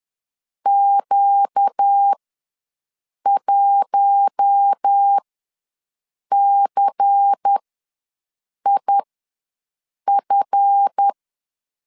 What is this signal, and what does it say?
Morse code, heard as a keyed tone, about 10 wpm, spaced as for 6 wpm, 800 Hz. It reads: Q1CIF
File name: LEZIONI / 7 GF9Z / 7 GF ws 10wpm 3918